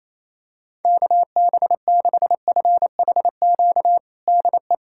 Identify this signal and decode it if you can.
Morse code, heard as a keyed tone, about 28 wpm, 700 Hz. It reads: KB6FHQ DE
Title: USA Callsign DE Callsign 28wpm